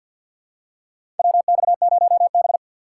Morse code, heard as a keyed tone, about 52 wpm, 700 Hz. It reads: WX0B
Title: USA Call Signs - 4 Characters - 6x Character Spacing - 50wpm